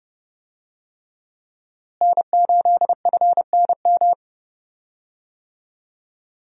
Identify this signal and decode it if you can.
Morse code, heard as a keyed tone, about 30 wpm, 700 Hz. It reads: N8FNM
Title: USA Call Signs - 5 Characters - Repeated 3x - 30wpm